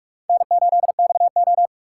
Morse code, heard as a keyed tone, about 44 wpm, 700 Hz. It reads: N8XO